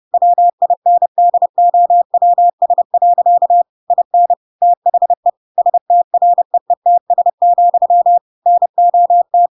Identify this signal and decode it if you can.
Morse code, heard as a keyed tone, about 30 wpm, 700 Hz. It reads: WINDOWS. IN THE STREETS, NOT
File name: Pinocchio / Pinocchio ch6 30 30 ik1hge